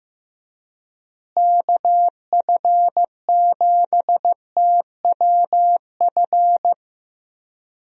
Morse code, heard as a keyed tone, about 16 wpm, 700 Hz. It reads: KF7TWF
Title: USA Call Signs - 6 Characters - Repeated 3x - 15wpm